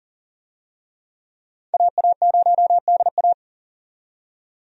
Morse code, heard as a keyed tone, about 40 wpm, 700 Hz. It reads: AA0DA
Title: USA Call Signs - 5 Characters - 40wpm